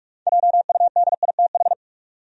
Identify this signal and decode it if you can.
Morse code, heard as a keyed tone, about 46 wpm, 700 Hz. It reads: JUDITH
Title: USA Common Names - 45wpm